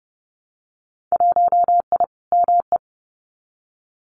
Morse code, heard as a keyed tone, about 30 wpm, 700 Hz. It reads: 1I ME
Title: Winter Field Day Exchanges Encore 30wpm